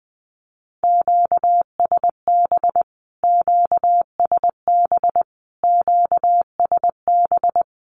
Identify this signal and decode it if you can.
Morse code, heard as a keyed tone, about 19 wpm, 700 Hz. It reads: QSB QSB QSB
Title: Q-Codes Repeated 3x 20wpm